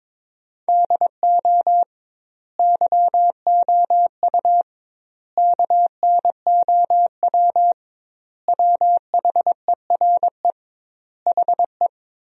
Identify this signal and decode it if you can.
Morse code, heard as a keyed tone, about 22 wpm, 700 Hz. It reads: DO YOU KNOW WHERE HE
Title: 2x Word-Spacing - Sentences from Top 300 Words - 22wpm